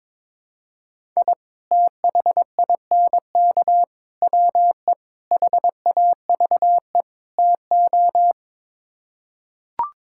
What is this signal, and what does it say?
Morse code, heard as a keyed tone, about 22 wpm, 700 Hz. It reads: I THINK WE HAVE TO
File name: practice / 22wpm / Build A Phrase - 5-Word Phrases - 22wpm